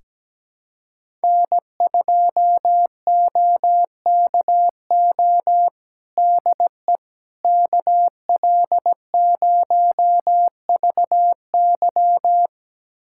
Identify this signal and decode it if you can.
Morse code, heard as a keyed tone, about 17 wpm, 700 Hz. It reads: N2OKO DE KL0VY
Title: USA Callsign DE Callsign 17wpm